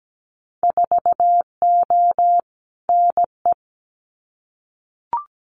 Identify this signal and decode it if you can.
Morse code, heard as a keyed tone, about 17 wpm, 700 Hz. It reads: 4O NE
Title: Winter Field Day Exchanges Encore 17wpm